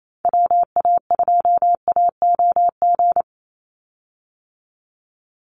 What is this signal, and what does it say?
Morse code, heard as a keyed tone, about 28 wpm, 700 Hz. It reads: WA2AOG
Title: Call Signs - 28wpm